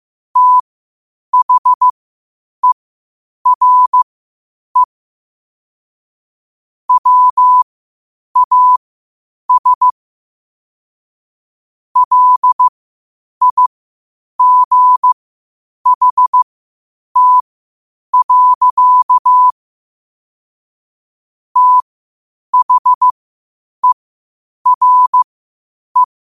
Morse code, heard as a keyed tone, about 15 wpm, spaced as for 8 wpm, 1000 Hz. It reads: THERE WAS LIGHT. THERE
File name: qotdfiles / 8wpm